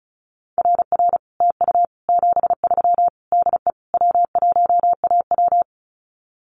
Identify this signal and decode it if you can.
Morse code, heard as a keyed tone, about 34 wpm, 700 Hz. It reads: RR TU 73 DE W1AW
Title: Common QSO Phrases Encore 35wpm